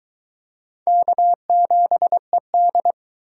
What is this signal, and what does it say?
Morse code, heard as a keyed tone, about 22 wpm, 700 Hz. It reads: K7ED